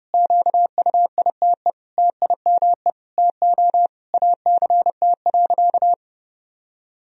Morse code, heard as a keyed tone, about 30 wpm, 700 Hz. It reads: QUITE TIME TO ACT.
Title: Sentences from Top 1000 Words - Review Part 1 - 30wpm